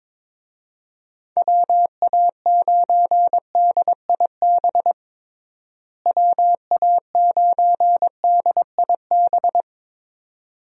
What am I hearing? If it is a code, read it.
Morse code, heard as a keyed tone, about 22 wpm, 700 Hz. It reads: WA9DIB WA9DIB
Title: USA Call Signs - 6 Characters - Repeated 2x - 22wpm